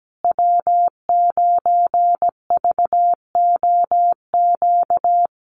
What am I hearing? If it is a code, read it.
Morse code, heard as a keyed tone, about 17 wpm, 700 Hz. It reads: W9VOQ
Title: USA Call Signs Repeated 3x 17wpm